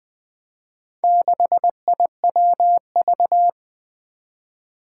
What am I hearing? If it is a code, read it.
Morse code, heard as a keyed tone, about 20 wpm, 700 Hz. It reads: 6IWV
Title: Four Challenging Letter-Number Combinations 20wpm